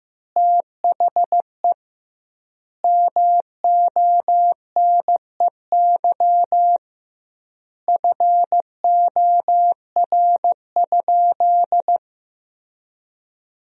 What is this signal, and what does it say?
Morse code, heard as a keyed tone, about 15 wpm, 700 Hz. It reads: THE MONEY FOR?
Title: 2x Word-Spacing - Sentences from Top 400 Words - 15wpm